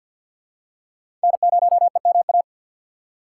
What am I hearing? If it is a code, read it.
Morse code, heard as a keyed tone, about 53 wpm, 700 Hz. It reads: N0EMA
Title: USA Call Signs - 5 Characters - 6x Character Spacing - 50wpm